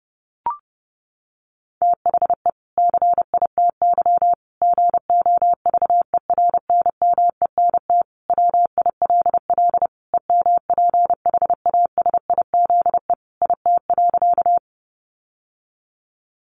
Morse code, heard as a keyed tone, about 30 wpm, 700 Hz. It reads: THE CITY GOVERNMENT WILL EMPHASIZE IT.
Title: Sentences from Top 1500 Words 30wpm